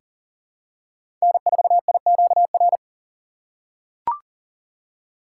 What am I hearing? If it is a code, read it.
Morse code, heard as a keyed tone, about 40 wpm, 700 Hz. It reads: N4IQR
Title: USA Call Signs - 5 Characters - 6x Character Spacing - 40wpm